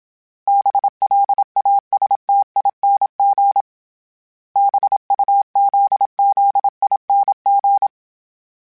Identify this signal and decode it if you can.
Morse code, heard as a keyed tone, about 27 wpm, 800 Hz. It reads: BLASTING BUZZING